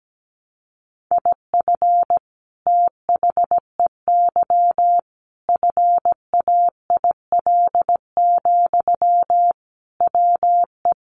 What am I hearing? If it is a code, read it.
Morse code, heard as a keyed tone, about 17 wpm, 700 Hz. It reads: IF THEY FAIL, WE
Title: Sentences from Top 1100 Words 17wpm